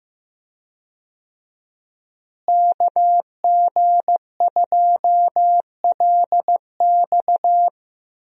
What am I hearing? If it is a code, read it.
Morse code, heard as a keyed tone, about 15 wpm, 700 Hz. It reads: KG2LX
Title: USA Call Signs - 5 Characters - Repeated 3x - 15wpm